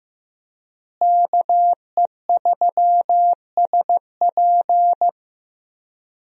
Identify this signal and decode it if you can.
Morse code, heard as a keyed tone, about 15 wpm, 700 Hz. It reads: KE3SP
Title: USA Call Signs - 5 Characters - Repeated 3x - 15wpm